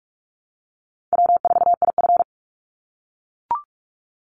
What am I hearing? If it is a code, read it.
Morse code, heard as a keyed tone, about 44 wpm, 700 Hz. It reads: W4IF
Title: Easy Call Signs - 45wpm